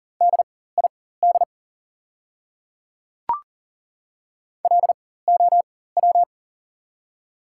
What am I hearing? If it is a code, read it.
Morse code, heard as a keyed tone, about 40 wpm, spaced as for 19 wpm, 700 Hz. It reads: DID LOW
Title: 4x Character Spacing - Top 200 Words - 40wpm